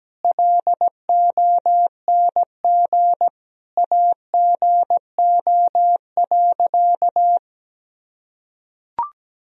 Morse code, heard as a keyed tone, about 17 wpm, 700 Hz. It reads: LONG AGO.
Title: Sentences from Top 1000 Words - Review Part 3 - 17wpm